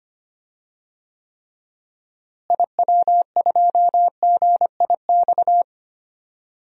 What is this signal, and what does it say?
Morse code, heard as a keyed tone, about 25 wpm, 700 Hz. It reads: IW2GIX
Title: DXCC-100 Call Signs - 6 Characters - Repeated 2x - 25wpm